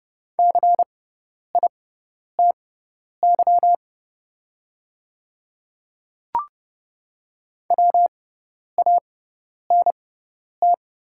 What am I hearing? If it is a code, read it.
Morse code, heard as a keyed tone, about 30 wpm, spaced as for 10 wpm, 690 Hz. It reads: CITY WANT
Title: 6x Character Spacing - Top 300 Words - 30wpm